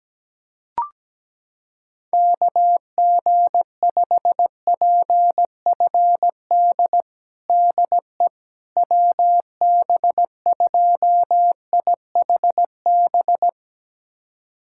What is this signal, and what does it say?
Morse code, heard as a keyed tone, about 17 wpm, 700 Hz. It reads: KG5PFD DE WB2IHB
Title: USA Callsign DE Callsign 17wpm